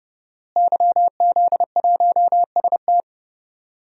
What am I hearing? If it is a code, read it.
Morse code, heard as a keyed tone, about 30 wpm, 700 Hz. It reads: YZ1ST
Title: DXCC-100 Call Signs - 5 Characters - Repeated 3x - 30wpm